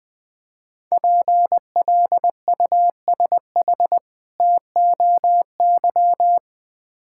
Morse code, heard as a keyed tone, about 20 wpm, 700 Hz. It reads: PLUSH TOY
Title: Difficult Words with Hints - 20wpm